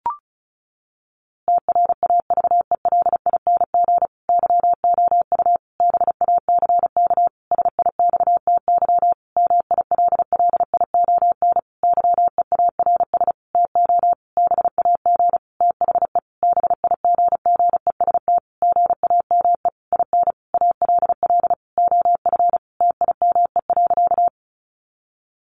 Morse code, heard as a keyed tone, about 34 wpm, 700 Hz. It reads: TRAVELING YOU BACK SIXTY MILLION YEARS TO BAG THE BIGGEST GAME IN ALL OF TIME.
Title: A Sound Of Thunder 35wpm